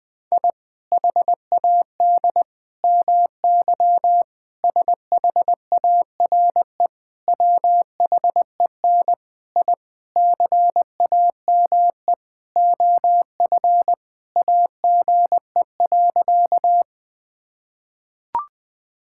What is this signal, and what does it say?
Morse code, heard as a keyed tone, about 19 wpm, 700 Hz. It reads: I HAD MY SHARE WHEN I CAME OF AGE.
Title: Sentences from Top 1000 Words - Review Part 6 - 20wpm